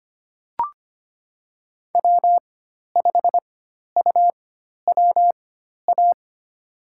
Morse code, heard as a keyed tone, about 25 wpm, spaced as for 12 wpm, 700 Hz. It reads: W5UWA
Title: USA Call Signs - 5 Characters - 4x Character Spacing - 25wpm